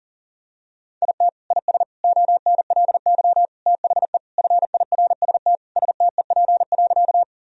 Morse code, heard as a keyed tone, about 40 wpm, 700 Hz. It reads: IT IS ONLY THE FIRST STEP.
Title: Sentences from Top 1000 Words - Review Part 2 - 40wpm